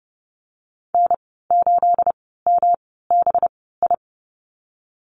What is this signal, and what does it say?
Morse code, heard as a keyed tone, about 30 wpm, spaced as for 17 wpm, 700 Hz. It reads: N8MBI
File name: Lesson 37g - Letter x - Call Signs 30wpm